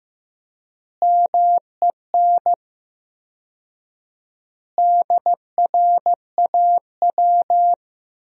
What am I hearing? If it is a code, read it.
Morse code, heard as a keyed tone, about 16 wpm, 700 Hz. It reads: MEN DRAW
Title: 4x Word-Spacing - Sentences from Top 300 Words - 15wpm